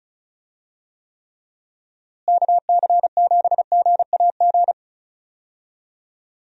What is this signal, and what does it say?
Morse code, heard as a keyed tone, about 34 wpm, 700 Hz. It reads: KC7GAG
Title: USA Call Signs - 6 Characters - Repeated 2x - 35wpm